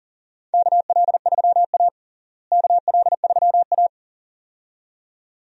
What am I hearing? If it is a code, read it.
Morse code, heard as a keyed tone, about 37 wpm, 700 Hz. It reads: KL3A KL3A
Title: USA Call Signs - 4 Characters - Repeated 2x - 40wpm